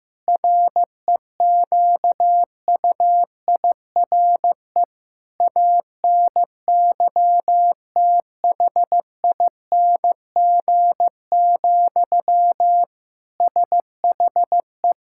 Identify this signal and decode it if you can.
Morse code, heard as a keyed tone, about 16 wpm, 700 Hz. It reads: REQUIRE ANYTHING, SHE
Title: Sentences from Top 1000 Words - Review Part 6 - 15wpm